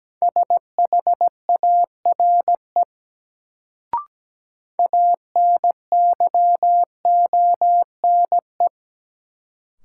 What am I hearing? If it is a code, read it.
Morse code, heard as a keyed tone, about 17 wpm, 700 Hz. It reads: SHARE ANYONE
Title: Sentences from Top 900 Words - New Word Drill - 17wpm